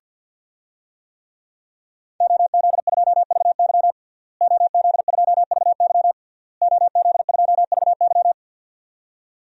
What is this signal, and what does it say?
Morse code, heard as a keyed tone, about 50 wpm, 700 Hz. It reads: OZ2VY OZ2VY OZ2VY
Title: DXCC-100 Call Signs - 5 Characters - Repeated 3x - 50wpm